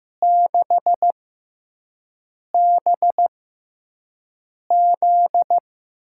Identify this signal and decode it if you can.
Morse code, heard as a keyed tone, about 15 wpm, spaced as for 5 wpm, 700 Hz. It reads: 6BZ
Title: USA Call Signs - 6 Characters - 6x Character Spacing - 15wpm